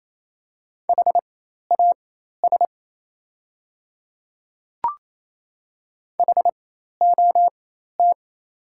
Morse code, heard as a keyed tone, about 27 wpm, spaced as for 13 wpm, 700 Hz. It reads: HAS HOT
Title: 4x Character Spacing - Top 200 Words - 28wpm